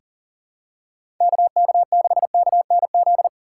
Morse code, heard as a keyed tone, about 40 wpm, 700 Hz. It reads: KK6KNZ
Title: USA Call Signs - 6 Characters - 3x Character Spacing - 40wpm